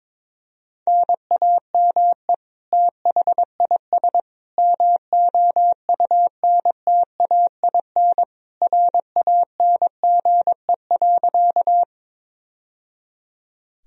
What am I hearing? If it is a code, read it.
Morse code, heard as a keyed tone, about 21 wpm, 700 Hz. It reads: NAME THIS MOUNTAIN RANGE.
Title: Sentences from Top 1000 Words - Review Part 4 - 22wpm